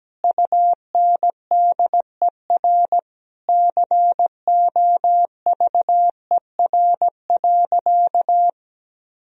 Morse code, heard as a keyed tone, about 17 wpm, 700 Hz. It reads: UNDER COVER.
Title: Sentences from Top 1000 Words - Review Part 1 - 17wpm